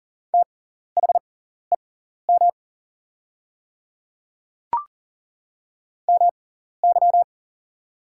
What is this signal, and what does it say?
Morse code, heard as a keyed tone, about 39 wpm, spaced as for 14 wpm, 700 Hz. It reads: THEM MY
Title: 6x Character Spacing - Top 100 Words - 40wpm